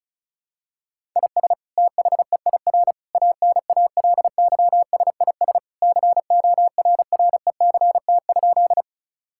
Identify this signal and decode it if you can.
Morse code, heard as a keyed tone, about 34 wpm, 700 Hz. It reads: IS THEIR ANALYSIS CORRECT?
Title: Sentences from Top 1000 Words - Review Part 5 - 35wpm